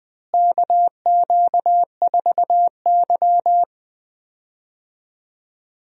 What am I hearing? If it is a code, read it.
Morse code, heard as a keyed tone, about 19 wpm, 700 Hz. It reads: KQ4Y
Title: USA Call Signs - 4 Characters - Repeated 2x - 20wpm